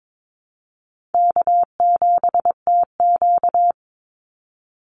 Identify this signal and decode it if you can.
Morse code, heard as a keyed tone, about 22 wpm, 700 Hz. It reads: K7TQ
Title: NA SOTA Chaser Callsigns 22wpm